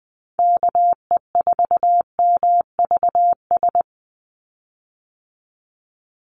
Morse code, heard as a keyed tone, about 20 wpm, 700 Hz. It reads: KE4MVS
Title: USA Call Signs 20wpm